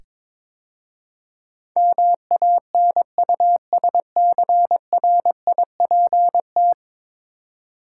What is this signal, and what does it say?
Morse code, heard as a keyed tone, about 22 wpm, 700 Hz. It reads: MANUSCRIPT
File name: practice / 22wpm / 10 Letter Words - First 3 Letters Given - 22wpm